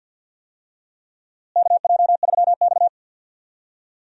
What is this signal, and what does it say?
Morse code, heard as a keyed tone, about 46 wpm, 700 Hz. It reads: KJ3X
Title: USA Call Signs - 4 Characters - Repeated 2x - 50wpm